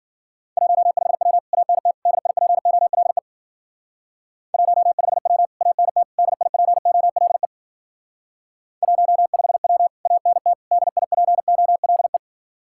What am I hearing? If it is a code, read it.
Morse code, heard as a keyed tone, about 48 wpm, 700 Hz. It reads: 15W ANT DIPOLE 15W ANT DIPOLE 15W ANT DIPOLE
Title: Speed-Racing Common QSO Phrases Encore - 40wpm